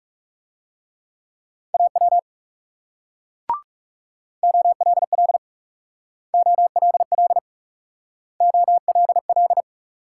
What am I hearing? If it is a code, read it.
Morse code, heard as a keyed tone, about 38 wpm, 700 Hz. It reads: AW OLL OLL OLL